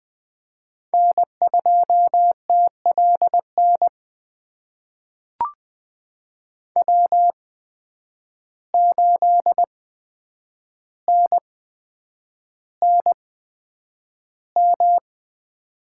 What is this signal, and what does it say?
Morse code, heard as a keyed tone, about 20 wpm, spaced as for 9 wpm, 700 Hz. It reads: N2TLN W8NNM
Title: USA Call Signs - 5 Characters - 8x Character Spacing - 20wpm